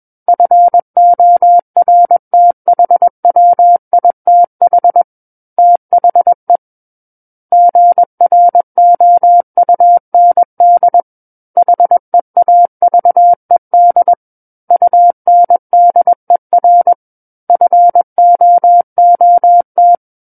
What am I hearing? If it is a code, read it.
Morse code, heard as a keyed tone, about 21 wpm, 700 Hz. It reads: FORTHWITH THE GROUND HEAVED UNDER FOOT
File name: Book One / 12-Chapter 12 - 21 WPM